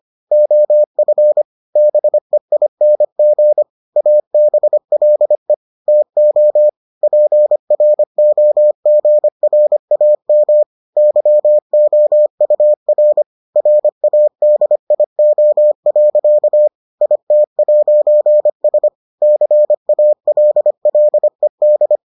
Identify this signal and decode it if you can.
Morse code, heard as a keyed tone, about 25 wpm, 600 Hz. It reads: OF BEING ABLE TO PROGRAM YOUR RADIO. IT'S CALLED